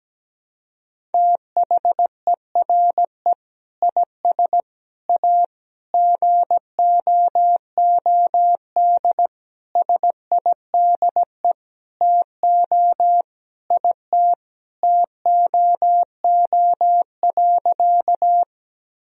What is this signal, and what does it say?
Morse code, heard as a keyed tone, about 17 wpm, 700 Hz. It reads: THERE IS A GOOD SIDE TO IT TOO.
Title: Sentences from Top 1000 Words - Review Part 1 - 17wpm